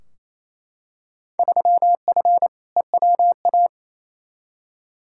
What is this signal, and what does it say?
Morse code, heard as a keyed tone, about 28 wpm, 700 Hz. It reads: 3F EWA
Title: Field Day Exchanges Encore - 28wpm